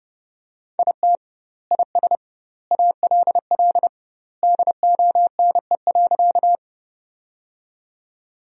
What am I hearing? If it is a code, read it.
Morse code, heard as a keyed tone, about 30 wpm, 700 Hz. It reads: IT IS ALL DONE.
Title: 2x Word-Spacing - Sentences from Top 500 Words - 30wpm